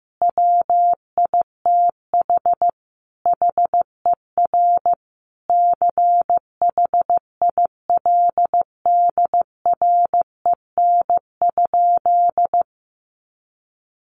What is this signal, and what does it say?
Morse code, heard as a keyed tone, about 15 wpm, 700 Hz. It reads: WITH HER CHILDREN?
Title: Sentences from Top 1500 Words 15wpm